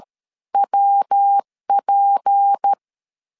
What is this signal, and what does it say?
Morse code, heard as a keyed tone, about 13 wpm, 800 Hz. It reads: WP
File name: CW / DET / 1 6 1500C S A 12wpm 3440